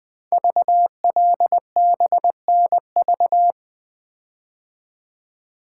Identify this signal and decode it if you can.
Morse code, heard as a keyed tone, about 20 wpm, 700 Hz. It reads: VLBNV
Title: Code Groups - 5 Letter:Numbers - 20wpm